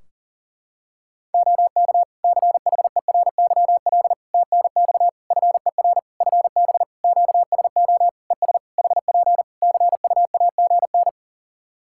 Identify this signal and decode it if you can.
Morse code, heard as a keyed tone, about 41 wpm, 700 Hz. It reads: OK CHERYL TNX FER FB QSO ES HP CUAGN
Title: Practice QSO - Exchange 4 - Ending - 40wpm